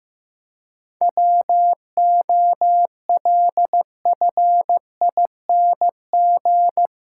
Morse code, heard as a keyed tone, about 15 wpm, 700 Hz. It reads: WOLFING
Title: Pangram Challenge v2 - 15wpm